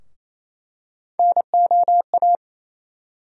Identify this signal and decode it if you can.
Morse code, heard as a keyed tone, about 28 wpm, 700 Hz. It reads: NOA
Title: USA Call Sign Postfixes - 3 Letters 28wpm